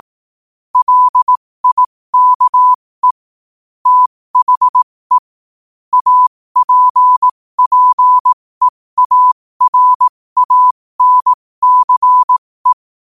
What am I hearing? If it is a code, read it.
Morse code, heard as a keyed tone, about 18 wpm, spaced as for 15 wpm, 1000 Hz. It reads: LIKE THE APPEARANCE
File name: qotdfiles / 15wpm